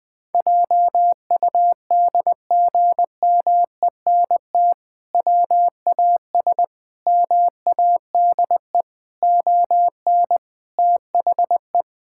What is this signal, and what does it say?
Morse code, heard as a keyed tone, about 20 wpm, 700 Hz. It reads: JUDGMENT WAS MADE ON THE
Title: Sentences with 5-Word Phrases - 20wpm